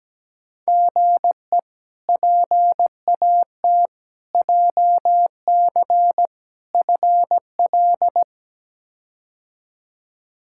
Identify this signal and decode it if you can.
Morse code, heard as a keyed tone, about 17 wpm, 700 Hz. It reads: GE PAT JC FL
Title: K1USN SST QSO Practice - 17wpm